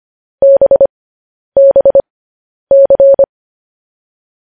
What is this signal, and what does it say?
Morse code, heard as a keyed tone, about 25 wpm, spaced as for 10 wpm, 550 Hz. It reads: BBC
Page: news headlines in morse code at 10 wpm